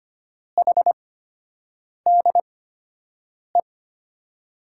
Morse code, heard as a keyed tone, about 25 wpm, spaced as for 7 wpm, 700 Hz. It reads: HDE
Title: USA Call Signs - 6 Characters - 8x Character Spacing - 25wpm